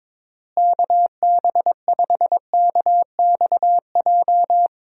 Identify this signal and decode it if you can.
Morse code, heard as a keyed tone, about 22 wpm, 700 Hz. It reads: KB5KXJ
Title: USA Call Signs - 6 Characters - Repeated 3x - 22wpm